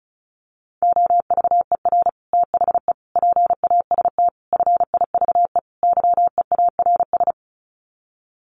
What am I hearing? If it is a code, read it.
Morse code, heard as a keyed tone, about 34 wpm, 700 Hz. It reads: OVER THE PAST FIVE YEARS
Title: Sets of 5 Words Encore 35wpm